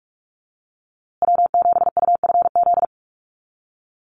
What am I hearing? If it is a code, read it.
Morse code, heard as a keyed tone, about 44 wpm, 700 Hz. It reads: W7UFZ
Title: USA Call Signs Repeated 3x 45wpm